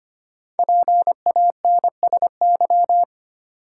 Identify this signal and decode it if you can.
Morse code, heard as a keyed tone, about 25 wpm, 700 Hz. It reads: PANSY
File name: practice / 25wpm / Difficult Words with Hints - 25wpm